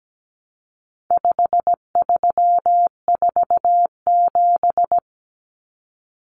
Morse code, heard as a keyed tone, about 17 wpm, 700 Hz. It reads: 5347
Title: Common Cut Numbers - 4-Digit 17wpm